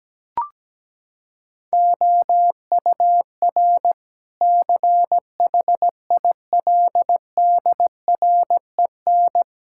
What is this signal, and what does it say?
Morse code, heard as a keyed tone, about 17 wpm, 700 Hz. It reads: OUR CHILDREN
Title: Sentences from Top 1000 Words - Review Part 5 - 17wpm